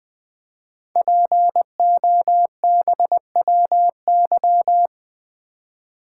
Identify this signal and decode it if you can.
Morse code, heard as a keyed tone, about 20 wpm, 700 Hz. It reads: POBWY
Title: Code Groups - 5 Letters - 20wpm